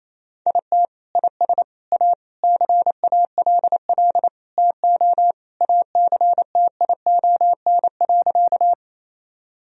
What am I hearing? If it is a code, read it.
Morse code, heard as a keyed tone, about 28 wpm, 700 Hz. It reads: IT IS A CALL TO ACTION.
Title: Sentences from Top 1000 Words - Review Part 4 - 28wpm